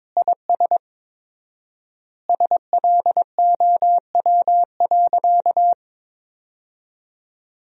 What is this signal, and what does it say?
Morse code, heard as a keyed tone, about 22 wpm, 700 Hz. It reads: IS SLOW.
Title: 4x Word-Spacing - Sentences from Top 400 Words - 22wpm